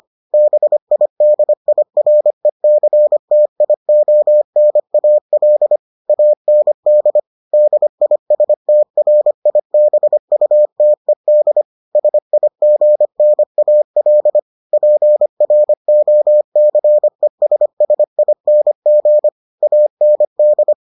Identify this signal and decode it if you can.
Morse code, heard as a keyed tone, about 25 wpm, 600 Hz. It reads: BIDIRECTIONAL AND DISTRIBUTED SIGNAL PROCESSING AND